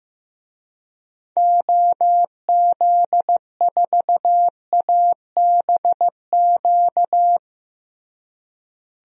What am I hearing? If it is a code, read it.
Morse code, heard as a keyed tone, about 15 wpm, 700 Hz. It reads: OZ4ABQ
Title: DXCC-100 Call Signs - 6 Characters - Repeated 2x - 15wpm